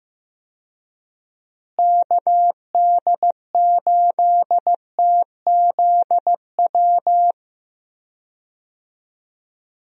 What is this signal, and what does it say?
Morse code, heard as a keyed tone, about 15 wpm, 700 Hz. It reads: KD8TZW
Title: USA Call Signs - 6 Characters - Repeated 3x - 15wpm